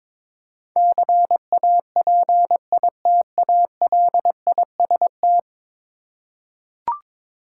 Morse code, heard as a keyed tone, about 22 wpm, 700 Hz. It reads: CAPITALIST
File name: practice / 22wpm / 10 Letter Words - First 3 Letters Given - 22wpm